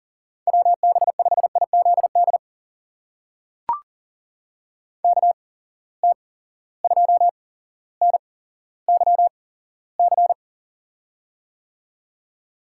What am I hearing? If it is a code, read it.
Morse code, heard as a keyed tone, about 41 wpm, spaced as for 17 wpm, 700 Hz. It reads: WB5IZD KT2NYC
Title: USA Call Signs - 6 Characters - 8x Character Spacing - 40wpm